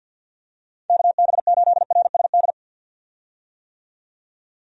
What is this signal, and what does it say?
Morse code, heard as a keyed tone, about 48 wpm, 700 Hz. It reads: KB8RSD
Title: USA Call Signs - 6 Characters - 50wpm